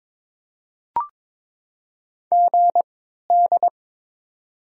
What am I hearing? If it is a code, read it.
Morse code, heard as a keyed tone, about 21 wpm, spaced as for 12 wpm, 700 Hz. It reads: GD
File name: Top 90 QSO Elements - 3x Character Spacing - 22wpm